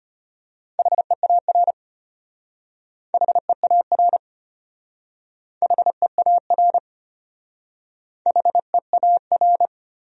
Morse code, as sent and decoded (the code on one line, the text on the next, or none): .... .- .-. / .... .- .-. / .... .- .-. / .... .- .-.
HEAR HEAR HEAR HEAR